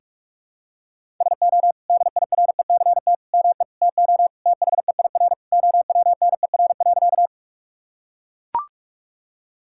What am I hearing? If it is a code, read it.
Morse code, heard as a keyed tone, about 41 wpm, 700 Hz. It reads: SO DIRECT ME TO THEIR OWNER.
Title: Sentences from Top 1600 Words 45wpm